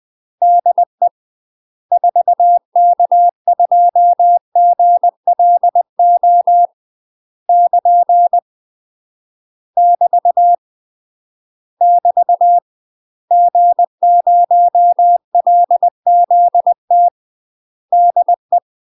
Morse code, heard as a keyed tone, about 20 wpm, 700 Hz. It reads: DE 4K2GLO <KN> <BT> <BT> G0LZT DE
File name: UK MASTER QSOs TRAINING/20WPM